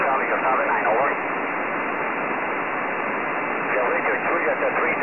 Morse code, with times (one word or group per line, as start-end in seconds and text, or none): none